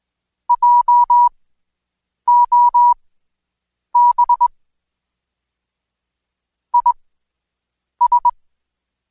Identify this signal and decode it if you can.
Morse code, heard as a keyed tone, about 21 wpm, spaced as for 7 wpm, 970 Hz. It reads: JOB IS
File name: Code Quick / AudioDownld2 / Seg112-Day17-DayDrill 17